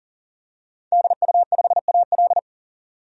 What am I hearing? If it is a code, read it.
Morse code, heard as a keyed tone, about 40 wpm, 700 Hz. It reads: DU5AL
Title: DXCC-100 Call Signs - 5 Characters - 40wpm